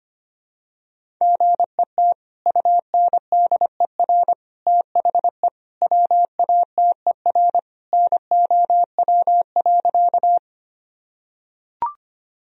Morse code, heard as a keyed tone, about 25 wpm, 700 Hz. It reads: GET UNDER THE WATER NOW.
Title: Sentences from Top 1000 Words - Review Part 1 - 25wpm